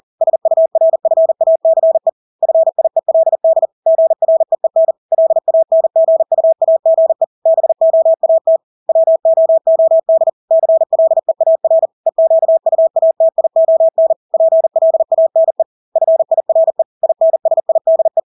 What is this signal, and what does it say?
Morse code, heard as a keyed tone, about 40 wpm, 650 Hz. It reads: SURFACE FIELD GREEN LANGUAGE BOAT WOOD CLEAR EQUATION PLANE FIRE INSIDE
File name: Engelske ord 40-40 wpm